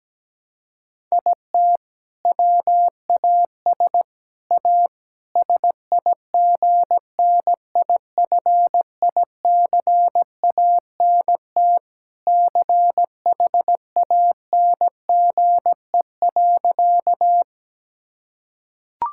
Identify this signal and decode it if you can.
Morse code, heard as a keyed tone, about 17 wpm, 700 Hz. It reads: IT WAS A SIGNIFICANT CHANGE.
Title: Sentences from Top 1000 Words - Review Part 6 - 17wpm